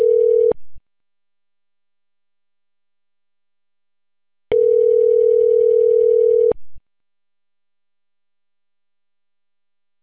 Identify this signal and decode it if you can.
Ringback tone (North America), 440 + 480 Hz.